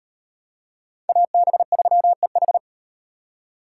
Morse code, heard as a keyed tone, about 35 wpm, 700 Hz. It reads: AB3EH